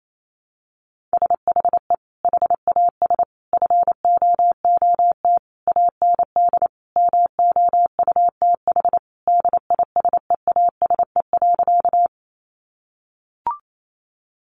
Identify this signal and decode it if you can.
Morse code, heard as a keyed tone, about 28 wpm, 700 Hz. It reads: SHE HAS FOOT AND MOUTH DISEASE.
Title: Sentences from Top 1500 Words 28wpm